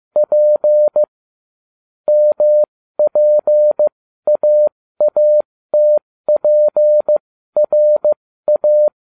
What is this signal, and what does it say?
Morse code, heard as a keyed tone, about 15 wpm, spaced as for 13 wpm, 600 Hz. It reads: P MPAATPRA